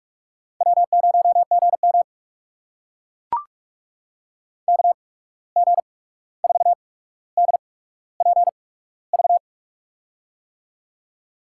Morse code, heard as a keyed tone, about 44 wpm, spaced as for 17 wpm, 700 Hz. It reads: W0GM KG4DPV
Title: USA Call Signs - 8x Character Spacing - 45wpm